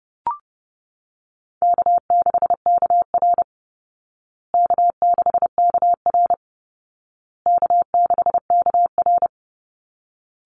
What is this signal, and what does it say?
Morse code, heard as a keyed tone, about 30 wpm, 700 Hz. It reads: K6KR K6KR K6KR
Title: Easy Call Signs Repeated 3x - 30wpm